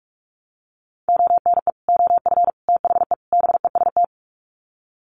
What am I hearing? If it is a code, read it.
Morse code, heard as a keyed tone, about 44 wpm, 700 Hz. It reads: ONE OF THE BEST
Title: Sets of 4 Words 45wpm